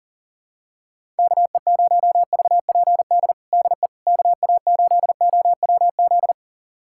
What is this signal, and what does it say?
Morse code, heard as a keyed tone, about 40 wpm, 690 Hz. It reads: KE0VPD DE KA8OWZ
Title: USA Callsign DE Callsign - 6 Characters - 40wpm